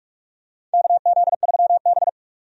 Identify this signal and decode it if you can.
Morse code, heard as a keyed tone, about 44 wpm, 700 Hz. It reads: KZ3B
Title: USA Call Signs - 4 Characters - 45wpm